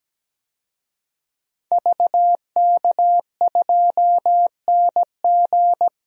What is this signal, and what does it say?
Morse code, heard as a keyed tone, about 17 wpm, 700 Hz. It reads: VK2NG
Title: DXCC-100 Call Signs - 5 Characters - 17wpm